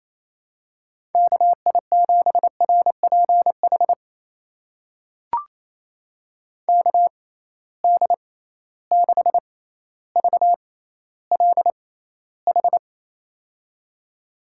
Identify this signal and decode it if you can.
Morse code, heard as a keyed tone, about 28 wpm, spaced as for 14 wpm, 700 Hz. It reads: KI7RPH KD6VLH
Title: USA Call Signs - 6 Characters - 6x Character Spacing - 28wpm